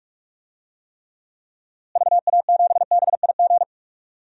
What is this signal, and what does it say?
Morse code, heard as a keyed tone, about 46 wpm, 700 Hz. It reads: VA7BIG
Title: DXCC-100 Call Signs - 6 Characters - Repeated 3x - 45wpm